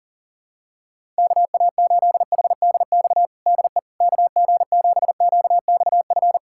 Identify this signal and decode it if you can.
Morse code, heard as a keyed tone, about 40 wpm, 700 Hz. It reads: KA8HDX DE KG7QXF
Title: USA Callsign DE Callsign - 6 Characters - 40wpm